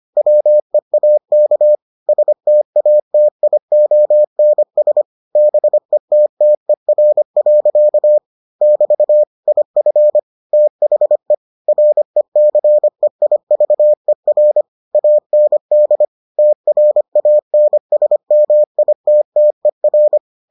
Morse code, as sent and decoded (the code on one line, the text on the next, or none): .-- . .- -.- / ... - .- - .. --- -. ... / -... - - . .-. .-.-.- / -...- / .. ..-. / - .... / .-. -.-. .. ...- . .-. / .- -. -.. / - .-. .- -. ... -- .. - - . .-.
WEAK STATIONS BETTER. <BT> IF THE RECEIVER AND TRANSMITTER